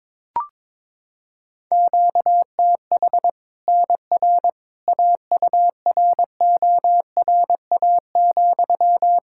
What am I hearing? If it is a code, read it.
Morse code, heard as a keyed tone, about 22 wpm, 700 Hz. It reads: QTH NR AURORA,
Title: Practice QSO - Exchange 1 - Beginning - 22wpm